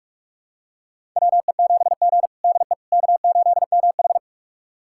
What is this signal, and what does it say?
Morse code, heard as a keyed tone, about 46 wpm, 700 Hz. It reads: WE7G DE K8MH
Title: USA Callsign DE Callsign - 4 Characters - 45wpm